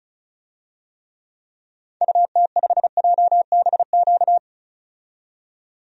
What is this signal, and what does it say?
Morse code, heard as a keyed tone, about 34 wpm, 700 Hz. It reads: UT5JBQ
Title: DXCC-100 Call Signs - 6 Characters - 35wpm